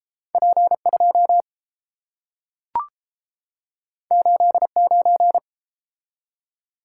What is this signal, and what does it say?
Morse code, heard as a keyed tone, about 30 wpm, 700 Hz. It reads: P2 89